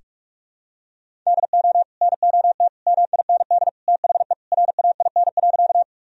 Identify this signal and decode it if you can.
Morse code, heard as a keyed tone, about 46 wpm, 700 Hz. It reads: DO NOT MIND THE RAIN.
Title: Sentences from Top 1000 Words - Review Part 1 - 45wpm